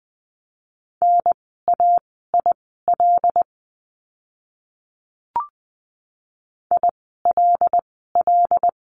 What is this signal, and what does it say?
Morse code, heard as a keyed tone, about 20 wpm, spaced as for 14 wpm, 700 Hz. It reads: NAIL ILL
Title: 075 Lesson 13f - Letter L - Words 20wpm